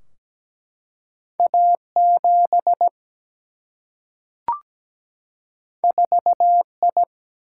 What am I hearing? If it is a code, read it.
Morse code, heard as a keyed tone, about 17 wpm, 700 Hz. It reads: A7 4I
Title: Single-Letter Single-Number 17wpm